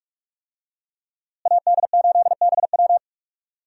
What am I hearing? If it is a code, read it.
Morse code, heard as a keyed tone, about 46 wpm, 700 Hz. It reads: AD8BW
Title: USA Call Signs - 5 Characters - Repeated 2x - 45wpm